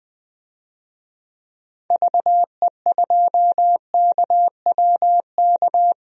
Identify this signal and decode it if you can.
Morse code, heard as a keyed tone, about 20 wpm, 700 Hz. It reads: VE2KWK
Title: DXCC-100 Call Signs - 6 Characters - 20wpm